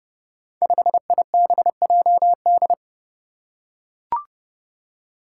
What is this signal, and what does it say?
Morse code, heard as a keyed tone, about 30 wpm, 700 Hz. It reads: 5IBJD
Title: Five Challenging Letter-Number Combinations 30wpm